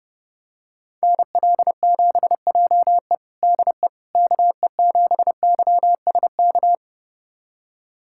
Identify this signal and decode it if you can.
Morse code, heard as a keyed tone, about 30 wpm, 700 Hz. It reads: NL7JE DE KE7YSK
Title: USA Callsign DE Callsign 30wpm